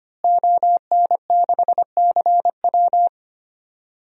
Morse code, heard as a keyed tone, about 25 wpm, 700 Hz. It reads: ON6CW
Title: DXCC-100 Call Signs - 5 Characters - Repeated 3x - 25wpm